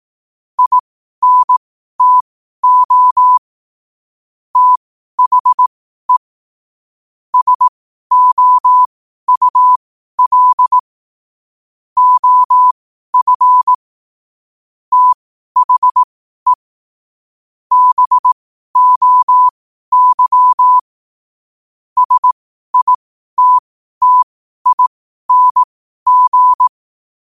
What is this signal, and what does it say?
Morse code, heard as a keyed tone, about 18 wpm, spaced as for 12 wpm, 1000 Hz. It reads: INTO THE SOUL OF THE BOY SITTING
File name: qotdfiles / 12wpm